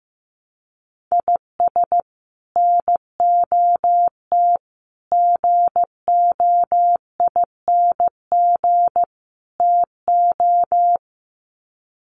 Morse code, heard as a keyed tone, about 15 wpm, 700 Hz. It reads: IS NOT GOING TO
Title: Sets of 4 Words 15wpm